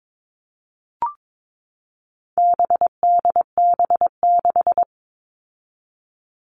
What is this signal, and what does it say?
Morse code, heard as a keyed tone, about 22 wpm, 700 Hz. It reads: BDB6
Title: All Cut Numbers - 4-Digit 22wpm